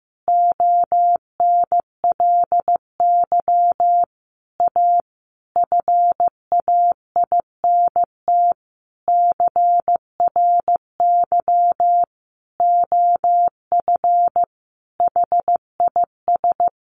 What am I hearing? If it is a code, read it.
Morse code, heard as a keyed tone, about 15 wpm, 700 Hz. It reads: ONLY A FAINT CRY OF HIS
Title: A Sound Of Thunder 15wpm